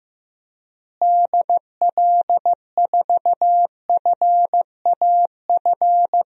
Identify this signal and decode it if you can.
Morse code, heard as a keyed tone, about 15 wpm, 700 Hz. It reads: DL4FAF
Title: DXCC-100 Call Signs - 6 Characters - Repeated 3x - 15wpm